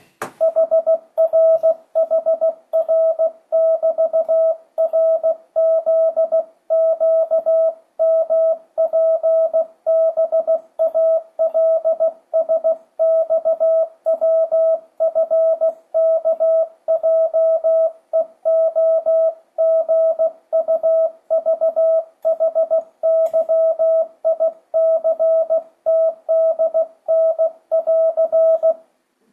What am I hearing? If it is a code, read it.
Morse code, heard as a keyed tone, about 16 wpm, 650 Hz. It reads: HRHR<BT>RZQMPBALSXWFKJEOGUVHYICTDN<AR>
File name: 2025cw / speedup / 16wpm